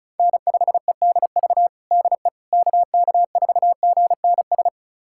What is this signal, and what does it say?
Morse code, heard as a keyed tone, about 36 wpm, 700 Hz. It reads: N5EDV DE KK4GNS
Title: USA Callsign DE Callsign 35wpm